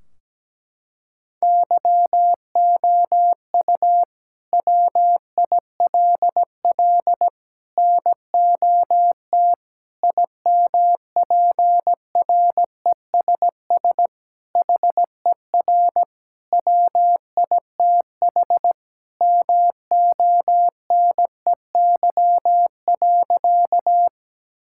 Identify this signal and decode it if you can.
Morse code, heard as a keyed tone, about 17 wpm, 700 Hz. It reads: YOU WILL NOT IMPRESS HER WITH MONEY.
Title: Sentences from Top 1600 Words 17wpm